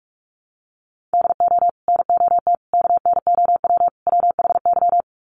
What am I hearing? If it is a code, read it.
Morse code, heard as a keyed tone, about 44 wpm, 700 Hz. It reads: DO NOT KNOW WHY
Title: Sets of 4 Words 45wpm